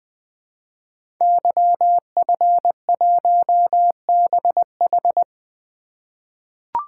Morse code, heard as a keyed tone, about 20 wpm, 700 Hz. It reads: YF1BH
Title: Code Groups - 5 Letter:Numbers - 20wpm